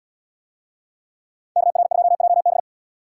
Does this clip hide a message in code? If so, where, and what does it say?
Yes: Morse code from 1.6 to 2.6 s, heard as a keyed tone, about 70 wpm, 700 Hz. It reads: KR2JB